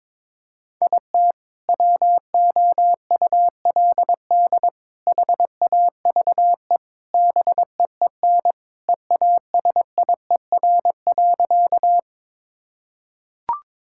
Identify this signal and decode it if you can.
Morse code, heard as a keyed tone, about 22 wpm, 700 Hz. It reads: IT WOULD HAVE BEEN EASIER.
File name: practice / 22wpm / Sentences with 3-Word Phrases - 22wpm